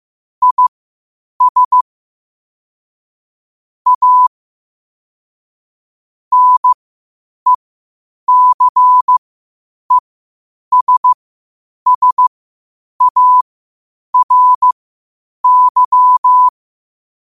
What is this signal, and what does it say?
Morse code, heard as a keyed tone, about 15 wpm, spaced as for 8 wpm, 1000 Hz. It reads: IS A NECESSARY
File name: qotdfiles / 8wpm